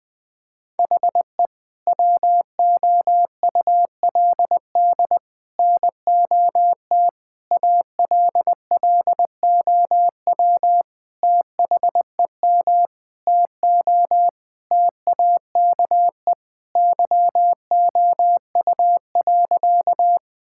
Morse code, heard as a keyed tone, about 20 wpm, 700 Hz. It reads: HE WOULD NOT ALLOW THEM TO TAKE YOU.
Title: Sentences from Top 1000 Words - Review Part 5 - 20wpm